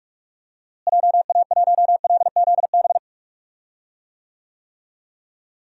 Morse code, heard as a keyed tone, about 46 wpm, 700 Hz. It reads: JA1LZB
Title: DXCC-100 Call Signs - 6 Characters - Repeated 2x - 45wpm